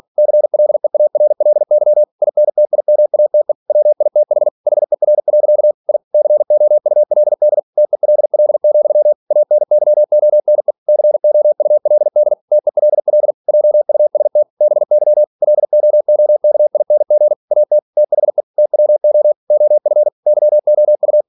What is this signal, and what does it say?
Morse code, heard as a keyed tone, about 46 wpm, 600 Hz. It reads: CLEARLY INTIMATE WITH HER. I COULD TELL, ANYONE COULD TELL JUST BY LOOKING AT THE TWO OF YOU
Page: Never 47wpm